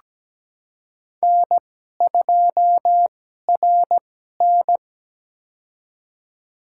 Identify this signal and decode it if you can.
Morse code, heard as a keyed tone, about 17 wpm, spaced as for 12 wpm, 700 Hz. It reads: N2RN
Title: USA Call Signs - 4 Characters - 2x Character Spacing - 17wpm